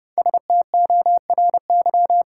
Morse code, heard as a keyed tone, about 30 wpm, 700 Hz. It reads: STORY
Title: Sentences from Top 300 Words - New Word Drill - 30wpm